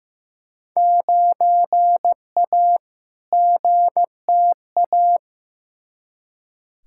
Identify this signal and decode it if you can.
Morse code, heard as a keyed tone, about 15 wpm, 700 Hz. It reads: 9A GTA
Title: Field Day Exchanges Encore - 15wpm